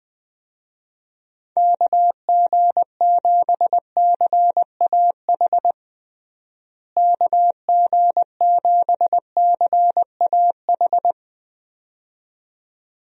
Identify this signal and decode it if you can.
Morse code, heard as a keyed tone, about 20 wpm, 700 Hz. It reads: KG7CAH KG7CAH
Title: USA Call Signs - 6 Characters - Repeated 2x - 20wpm